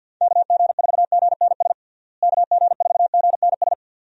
Morse code, heard as a keyed tone, about 51 wpm, 700 Hz. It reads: KG4GNS KG4GNS
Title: USA Call Signs - 6 Characters - Repeated 2x - 50wpm